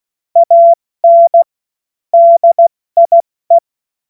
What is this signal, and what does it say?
Morse code, heard as a keyed tone, about 16 wpm, 680 Hz. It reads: AN DIE
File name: Morsekurs / Witze / Witz 28. September 2021, 70 F80